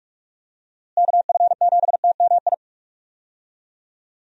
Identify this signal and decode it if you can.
Morse code, heard as a keyed tone, about 42 wpm, 700 Hz. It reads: KF7TMI
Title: USA Call Signs - 6 Characters - Repeated 3x - 45wpm